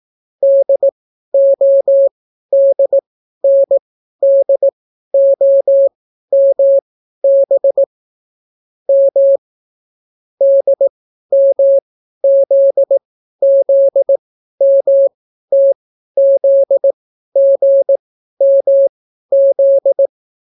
Morse code, heard as a keyed tone, about 18 wpm, spaced as for 12 wpm, 550 Hz. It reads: DODNDOMB M DMZZMTZGMZ